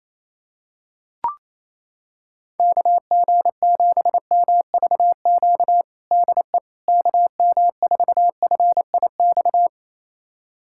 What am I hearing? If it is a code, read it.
Morse code, heard as a keyed tone, about 28 wpm, 700 Hz. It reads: KG7MVQ DE KM4FIX
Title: USA Callsign DE Callsign 28wpm